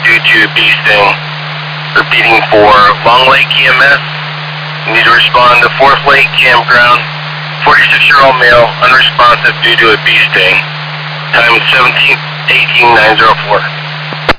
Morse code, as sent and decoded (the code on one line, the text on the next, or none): none